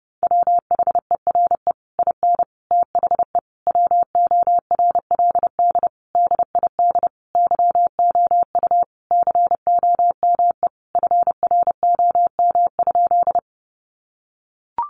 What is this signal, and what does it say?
Morse code, heard as a keyed tone, about 30 wpm, 700 Hz. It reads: WHERE IN THE WORLD DID YOU COME FROM?
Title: Sentences from Top 200 Words 30wpm